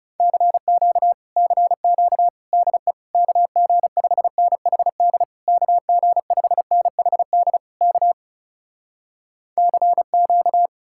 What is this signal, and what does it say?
Morse code, heard as a keyed tone, about 40 wpm, 700 Hz. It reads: CQ CQ DE KG5NHD KG5NHD K CQ